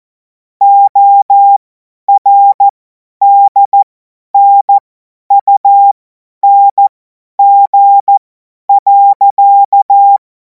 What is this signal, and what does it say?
Morse code, heard as a keyed tone, about 14 wpm, spaced as for 10 wpm, 800 Hz. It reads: ORDNUNG.
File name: 50 F70